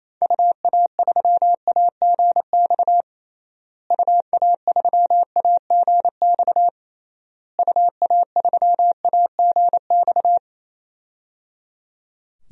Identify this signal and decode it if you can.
Morse code, heard as a keyed tone, about 28 wpm, 700 Hz. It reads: UA3AGX UA3AGX UA3AGX
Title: DXCC-100 Call Signs - 6 Characters - Repeated 3x - 28wpm